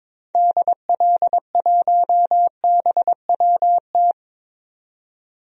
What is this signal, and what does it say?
Morse code, heard as a keyed tone, about 22 wpm, 700 Hz. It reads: DL1BWT
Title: DXCC-100 Call Signs - 6 Characters - 22wpm